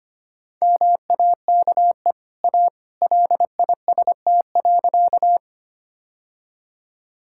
Morse code, heard as a keyed tone, about 25 wpm, 690 Hz. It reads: MAKE A LIST.
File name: Easy Sentences with Hints - 25wpm